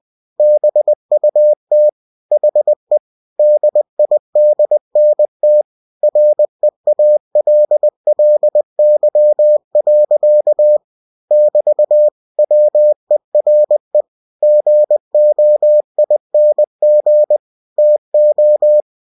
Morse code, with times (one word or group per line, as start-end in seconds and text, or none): 0.4-1.9 BUT
2.3-3.0 HE
3.4-5.6 DIDNT
6.0-10.8 REALLY.
11.3-12.1 <BT>
12.4-14.0 WERE
14.4-17.4 GOING
17.8-18.8 TO